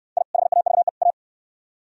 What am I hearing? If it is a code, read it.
Morse code, heard as a keyed tone, about 67 wpm, 700 Hz. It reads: I HAVE A